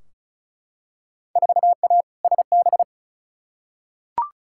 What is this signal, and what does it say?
Morse code, heard as a keyed tone, about 36 wpm, 700 Hz. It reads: 4A SB
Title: Field Day Exchanges Encore - 35wpm